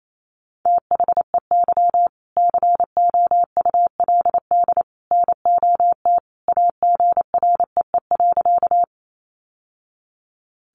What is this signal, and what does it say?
Morse code, heard as a keyed tone, about 28 wpm, 700 Hz. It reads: THEY COULD NOT AGREE.
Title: Sentences from Top 1100 Words 28wpm